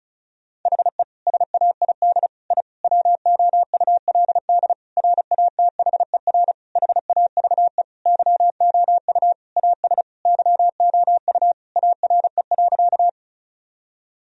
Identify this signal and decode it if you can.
Morse code, heard as a keyed tone, about 36 wpm, 700 Hz. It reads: HE SAID I WOULD RATHER HAVE YOU AS YOU ARE.
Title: Sentences from Top 1000 Words - Review Part 4 - 35wpm